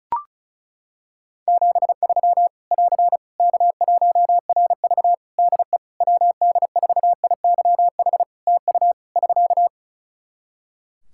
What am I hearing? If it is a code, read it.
Morse code, heard as a keyed tone, about 36 wpm, 700 Hz. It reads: 73 <AR> K1RV DE WD4IYH TU <SK>
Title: Practice QSO - Exchange 4 - Ending - 35wpm